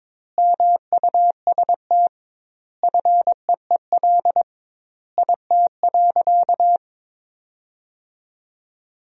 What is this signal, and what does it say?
Morse code, heard as a keyed tone, about 22 wpm, 700 Hz. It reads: MUST FEEL IT.
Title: 2x Word-Spacing - Sentences from Top 400 Words - 22wpm